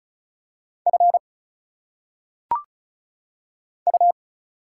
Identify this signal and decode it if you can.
Morse code, heard as a keyed tone, about 35 wpm, 700 Hz. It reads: F U
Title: Single Letters - Phonetic Pronunciation 35wpm